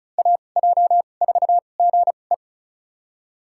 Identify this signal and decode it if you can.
Morse code, heard as a keyed tone, about 34 wpm, spaced as for 25 wpm, 700 Hz. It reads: AJ4GE
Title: USA Call Signs - 2x Character Spacing - 35wpm